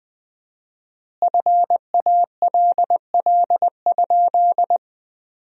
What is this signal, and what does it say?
Morse code, heard as a keyed tone, about 20 wpm, 700 Hz. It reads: FALL?
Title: 4x Word-Spacing - Sentences from Top 500 Words - 20wpm